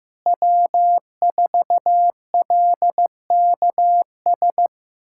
Morse code, heard as a keyed tone, about 15 wpm, 700 Hz. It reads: W4LKS
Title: USA Callsign DE Callsign 15wpm